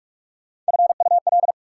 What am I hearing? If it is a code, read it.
Morse code, heard as a keyed tone, about 60 wpm, spaced as for 53 wpm, 700 Hz. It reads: FUL
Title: Speed-Racing Prefixes and Suffixes 45wpm